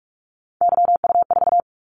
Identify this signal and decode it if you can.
Morse code, heard as a keyed tone, about 46 wpm, 700 Hz. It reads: YU4
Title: Callsign Prefixes for Top 25 DXCC - 45wpm